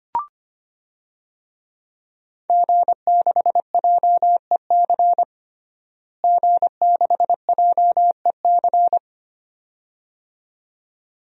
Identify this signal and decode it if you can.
Morse code, heard as a keyed tone, about 25 wpm, 700 Hz. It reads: G6JEC G6JEC
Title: DXCC-100 Call Signs - 5 Characters - Repeated 2x - 25wpm